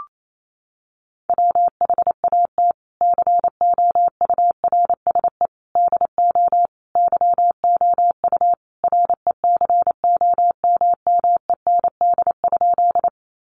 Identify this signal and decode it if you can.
Morse code, heard as a keyed tone, about 28 wpm, 700 Hz. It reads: WHAT COURSE DO YOU RECOMMEND?
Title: Sentences from Top 1200 Words 28wpm